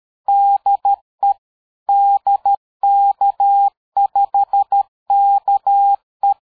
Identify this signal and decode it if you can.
Morse code, heard as a keyed tone, about 13 wpm, 800 Hz. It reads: DE DK5KE
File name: dk5ke / QSO 60BpM